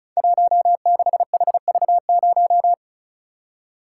Morse code, heard as a keyed tone, about 36 wpm, 700 Hz. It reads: 16HV0
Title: Code Groups - 5 Letter:Numbers - 35wpm